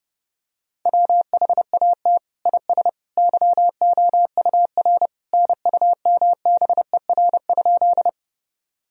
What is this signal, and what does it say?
Morse code, heard as a keyed tone, about 30 wpm, 700 Hz. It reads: WHAT IS YOUR NUMBER?
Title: Sentences from Top 1000 Words - Review Part 1 - 30wpm